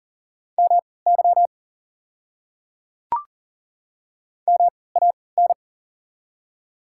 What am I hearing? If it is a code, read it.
Morse code, heard as a keyed tone, about 40 wpm, spaced as for 23 wpm, 700 Hz. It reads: MY MAN
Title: 3x Character Spacing - Top 500 Words - 40wpm